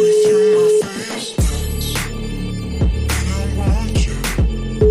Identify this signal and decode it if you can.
Ringback tone (Europe, 425 Hz), with noise.